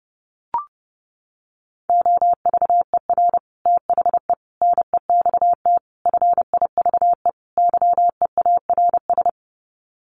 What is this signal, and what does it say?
Morse code, heard as a keyed tone, about 30 wpm, 700 Hz. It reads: OVER THE NEXT FIVE YEARS
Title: Sets of 5 Words 30wpm